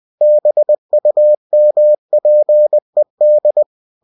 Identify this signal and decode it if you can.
Morse code, heard as a keyed tone, about 20 wpm, 600 Hz. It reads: BUMPED
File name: cw-books / winnie-the-pooh / winnie-20wpm-page-0109